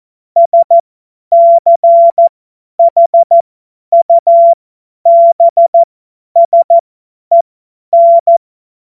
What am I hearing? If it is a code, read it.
Morse code, heard as a keyed tone, about 14 wpm, spaced as for 10 wpm, 680 Hz. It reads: SCHUBSEN
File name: Morsekurs / Witze / Witz 14. Februar 2023 50 F70